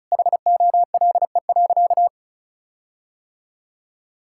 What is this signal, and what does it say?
Morse code, heard as a keyed tone, about 36 wpm, 700 Hz. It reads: HOLE.
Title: Sentences from Top 1600 Words 35wpm